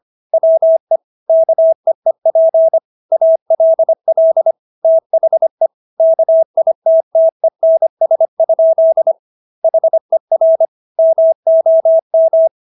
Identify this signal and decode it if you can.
Morse code, heard as a keyed tone, about 25 wpm, 650 Hz. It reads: WE KEEP ALL THE KITTENS? HER MOM